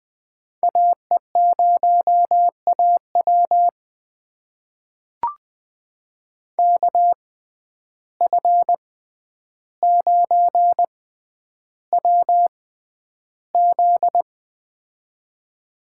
Morse code, heard as a keyed tone, about 20 wpm, spaced as for 10 wpm, 700 Hz. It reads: AE0AW KF9WZ
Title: USA Call Signs - 5 Characters - 6x Character Spacing - 20wpm